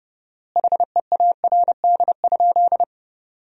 Morse code, heard as a keyed tone, about 30 wpm, 700 Hz. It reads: HEARD?
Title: 4x Word-Spacing - Sentences from Top 400 Words - 30wpm